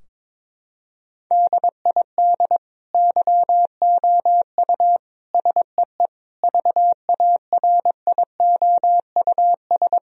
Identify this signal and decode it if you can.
Morse code, heard as a keyed tone, about 21 wpm, 700 Hz. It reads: DID YOU SEE VARIOUS
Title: Sentences from Top 1000 Words - Review Part 4 - 22wpm